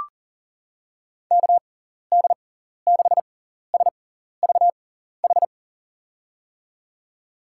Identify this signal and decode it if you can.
Morse code, heard as a keyed tone, about 40 wpm, spaced as for 14 wpm, 700 Hz. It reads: KD6SVH
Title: USA Call Signs - 6 Characters - 6x Character Spacing - 40wpm